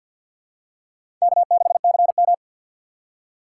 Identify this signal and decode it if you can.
Morse code, heard as a keyed tone, about 54 wpm, 700 Hz. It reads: K6CM
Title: USA Call Signs - 4 Characters - 3x Character Spacing - 50wpm